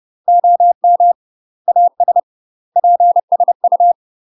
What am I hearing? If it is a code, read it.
Morse code, heard as a keyed tone, about 30 wpm, 700 Hz. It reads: OM AS PSU
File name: UK MASTER QSOs TRAINING / 30WPM / UK MASTER QSOs 0016